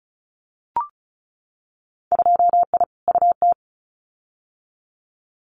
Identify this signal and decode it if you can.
Morse code, heard as a keyed tone, about 36 wpm, 700 Hz. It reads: 2I UT
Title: Winter Field Day Exchanges 35wpm